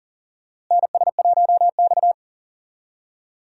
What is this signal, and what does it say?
Morse code, heard as a keyed tone, about 40 wpm, 700 Hz. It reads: NS1X
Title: USA Call Signs - 4 Characters - Repeated 2x - 40wpm